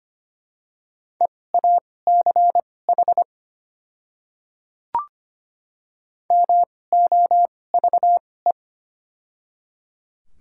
Morse code, heard as a keyed tone, about 25 wpm, spaced as for 18 wpm, 700 Hz. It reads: EACH MOVE
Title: 2x Character Spacing - Top 200 Words - 25wpm